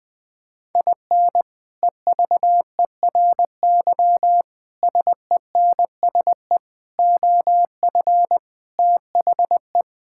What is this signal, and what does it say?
Morse code, heard as a keyed tone, about 20 wpm, 700 Hz. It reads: IN EVERY SENSE OF THE
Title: Sets of 6 Words - 20wpm